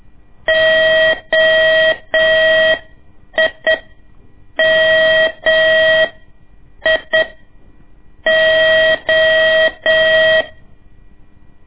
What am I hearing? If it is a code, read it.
Morse code, heard as a keyed tone, about 8 wpm, spaced as for 6 wpm, 640 Hz. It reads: OIMIO
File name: ik1weg2 / CORSI CW / lez2-1